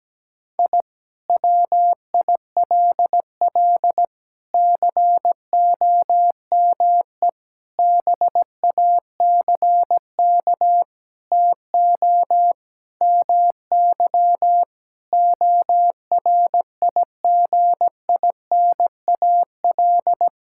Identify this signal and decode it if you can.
Morse code, heard as a keyed tone, about 17 wpm, 700 Hz. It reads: I WILL COME BACK TO MY ORIGINAL
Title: Sentences from Top 1000 Words - Review Part 6 - 17wpm